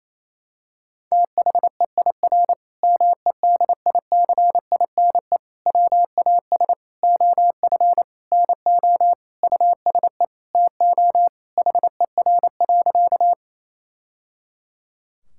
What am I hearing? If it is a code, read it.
Morse code, heard as a keyed tone, about 27 wpm, 700 Hz. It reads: THEIR MEDICINE WAS OF NO USE TO HER.
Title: Sentences from Top 1000 Words - Review Part 6 - 28wpm